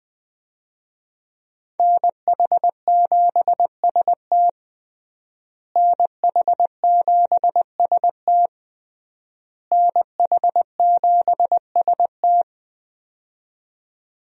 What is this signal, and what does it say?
Morse code, heard as a keyed tone, about 20 wpm, 700 Hz. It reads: NH7ST NH7ST NH7ST
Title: USA Call Signs - 5 Characters - Repeated 3x - 20wpm